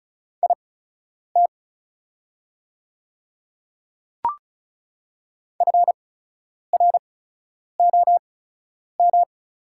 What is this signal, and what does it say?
Morse code, heard as a keyed tone, about 36 wpm, spaced as for 10 wpm, 690 Hz. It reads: IT FROM
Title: 8x Character Spacing - Top 200 Words - 35wpm